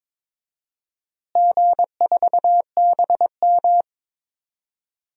Morse code, heard as a keyed tone, about 22 wpm, 700 Hz. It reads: G4BM
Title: DXCC-100 Call Signs - 4 Characters - Repeated 2x - 22wpm